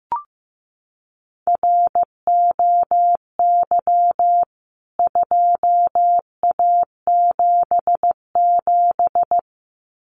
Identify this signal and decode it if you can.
Morse code, heard as a keyed tone, about 15 wpm, 700 Hz. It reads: ROY 2A77
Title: CWT Exchange 15wpm